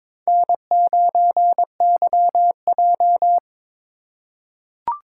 Morse code, heard as a keyed tone, about 22 wpm, 700 Hz. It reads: N9YJ
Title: USA Call Signs - 4 Characters - 2x Character Spacing - 22wpm